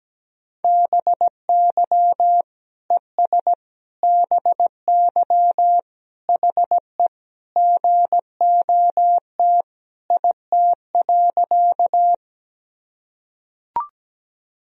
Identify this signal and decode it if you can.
Morse code, heard as a keyed tone, about 17 wpm, 700 Hz. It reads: BY ES BY HE GOT IT.
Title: Sentences from Top 300 Words and Abbreviations 17wpm